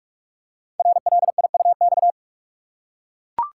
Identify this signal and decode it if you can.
Morse code, heard as a keyed tone, about 44 wpm, 700 Hz. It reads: RLIUX
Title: Code Groups - 5 Letter:Numbers - 45wpm